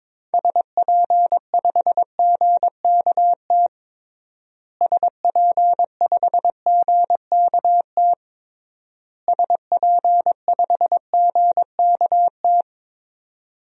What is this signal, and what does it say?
Morse code, heard as a keyed tone, about 22 wpm, 700 Hz. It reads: SP5GKT SP5GKT SP5GKT
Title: DXCC-100 Call Signs - 6 Characters - Repeated 3x - 22wpm